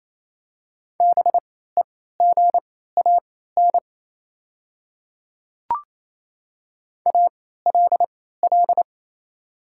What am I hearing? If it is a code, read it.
Morse code, heard as a keyed tone, about 27 wpm, spaced as for 16 wpm, 700 Hz. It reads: BEGAN ALL
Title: 3x Character Spacing - Top 300 Words - 28wpm